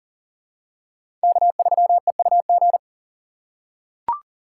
Morse code, heard as a keyed tone, about 40 wpm, 700 Hz. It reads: K3EUG
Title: USA Call Signs - 5 Characters - 2x Character Spacing - 40wpm